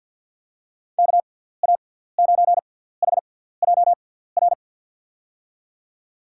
Morse code, heard as a keyed tone, about 49 wpm, spaced as for 17 wpm, 700 Hz. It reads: KA9HJR
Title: USA Call Signs - 6 Characters - 6x Character Spacing - 50wpm